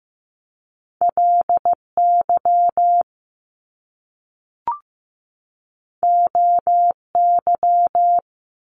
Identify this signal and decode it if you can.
Morse code, heard as a keyed tone, about 15 wpm, 700 Hz. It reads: LY OY
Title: Top Two Letter Combinations in Words 15wpm